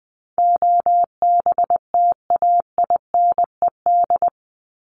Morse code, heard as a keyed tone, about 20 wpm, 700 Hz. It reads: OBTAINED
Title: ED Suffix - Top 100 Words with Base Words 20wpm